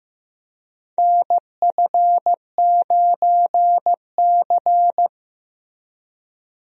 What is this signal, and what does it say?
Morse code, heard as a keyed tone, about 15 wpm, 700 Hz. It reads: NF9C
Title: USA Call Signs - 4 Characters - Repeated 3x - 15wpm